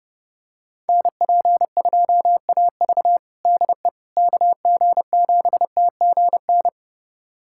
Morse code, heard as a keyed tone, about 32 wpm, 700 Hz. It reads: NP2AV DE KG7TGN
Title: USA Callsign DE Callsign 30wpm